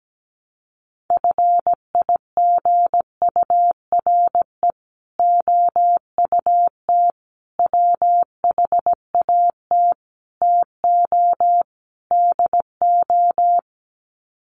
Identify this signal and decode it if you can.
Morse code, heard as a keyed tone, about 17 wpm, 700 Hz. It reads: FIGURE OUT WHAT TO DO
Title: Sets of 5 Words Encore2 17wpm